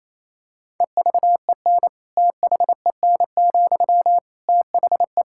Morse code, heard as a keyed tone, about 27 wpm, 700 Hz. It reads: EVEN THEN, THE
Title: Sentences from Top 1600 Words 28wpm